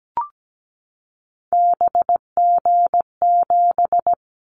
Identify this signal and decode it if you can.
Morse code, heard as a keyed tone, about 17 wpm, 700 Hz. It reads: BG7
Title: Callsign Prefixes for Top 100 DXCC - 17wpm